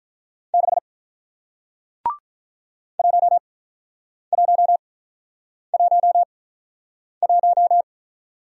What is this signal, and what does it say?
Morse code, heard as a keyed tone, about 47 wpm, spaced as for 9 wpm, 700 Hz. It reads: 6 1111